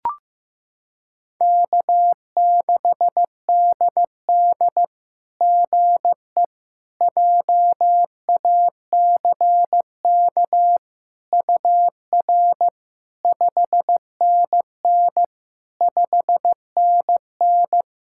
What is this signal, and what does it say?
Morse code, heard as a keyed tone, about 15 wpm, 700 Hz. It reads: K6DD GE JACK UR 5NN 5NN
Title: SOTA QSO Practice 15wpm